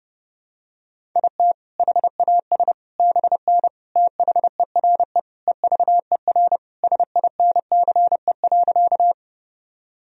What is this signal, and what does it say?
Morse code, heard as a keyed tone, about 30 wpm, 700 Hz. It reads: IT HAS BN THERE EVER SINCE.
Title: Sentences from Top 400 Words and Abbreviations 30wpm